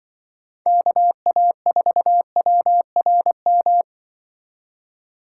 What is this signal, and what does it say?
Morse code, heard as a keyed tone, about 24 wpm, 700 Hz. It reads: KA4WRM
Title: Speed Racing - USA 2x3 Call Signs 20wpm